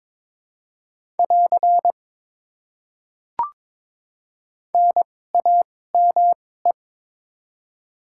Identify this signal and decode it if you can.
Morse code, heard as a keyed tone, about 22 wpm, spaced as for 16 wpm, 700 Hz. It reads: <AR> NAME
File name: Top 90 QSO Elements - 2x Character Spacing - 22wpm